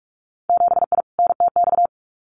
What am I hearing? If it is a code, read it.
Morse code, heard as a keyed tone, about 46 wpm, 700 Hz. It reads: 7I NTX
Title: Winter Field Day Exchanges Encore 45wpm